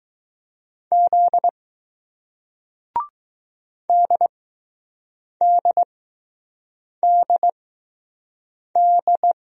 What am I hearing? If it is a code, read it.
Morse code, heard as a keyed tone, about 20 wpm, spaced as for 6 wpm, 700 Hz. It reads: Z DDDD